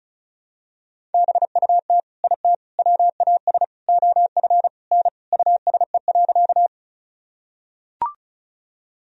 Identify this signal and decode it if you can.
Morse code, heard as a keyed tone, about 34 wpm, 700 Hz. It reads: BUT IT WAS OF N USE.
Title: Sentences from Top 100 Words and Abbreviations 35wpm